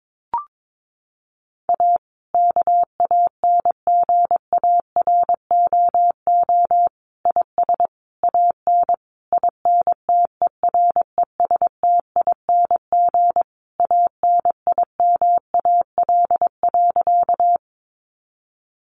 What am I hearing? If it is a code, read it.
Morse code, heard as a keyed tone, about 22 wpm, 700 Hz. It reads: A KANGAROO IS AN INTERESTING ANIMAL.
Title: Easy Phrases 22wpm